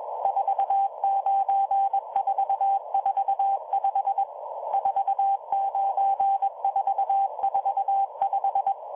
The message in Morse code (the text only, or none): 49445 49445